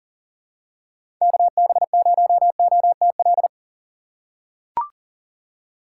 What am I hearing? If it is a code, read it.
Morse code, heard as a keyed tone, about 39 wpm, 700 Hz. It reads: KB0OTL
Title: USA Call Signs - 6 Characters - 4x Character Spacing - 40wpm